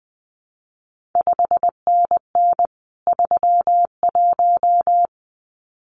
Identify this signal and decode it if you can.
Morse code, heard as a keyed tone, about 19 wpm, 700 Hz. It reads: 5NN 31
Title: CQ WW DX Contest Exchange - 20wpm